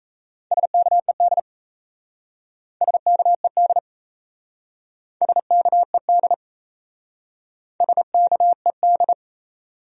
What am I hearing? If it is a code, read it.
Morse code, heard as a keyed tone, about 36 wpm, 700 Hz. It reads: SKED SKED SKED SKED